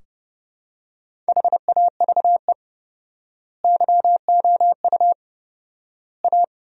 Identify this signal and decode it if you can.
Morse code, heard as a keyed tone, about 30 wpm, 700 Hz. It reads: HAVE YOU A
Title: 4x Word-Spacing - Sentences from Top 400 Words - 30wpm